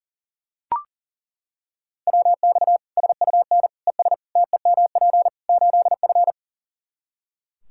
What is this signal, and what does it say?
Morse code, heard as a keyed tone, about 37 wpm, 700 Hz. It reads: WX SUN ES TEMP 8F
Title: Practice QSO - Exchange 2 - Info - 40wpm